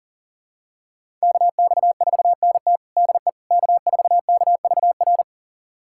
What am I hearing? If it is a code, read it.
Morse code, heard as a keyed tone, about 40 wpm, 700 Hz. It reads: KX4NT DE K4KVR
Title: USA Callsign DE Callsign - 5 Characters - 40wpm